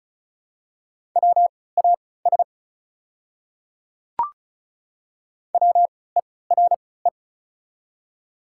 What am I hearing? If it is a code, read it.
Morse code, heard as a keyed tone, about 36 wpm, spaced as for 20 wpm, 690 Hz. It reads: WAS WERE
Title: 3x Character Spacing - Top 100 Words - 35wpm